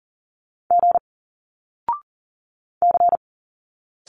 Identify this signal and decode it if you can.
Morse code, heard as a keyed tone, about 40 wpm, 700 Hz. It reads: G C
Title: Single Letters 40wpm